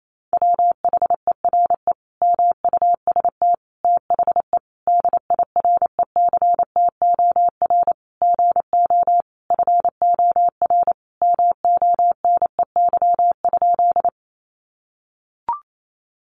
Morse code, heard as a keyed tone, about 28 wpm, 700 Hz. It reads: WHERE MUST THE DIRECTOR GO FOR MONEY?
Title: Sentences from Top 1300 Words 28wpm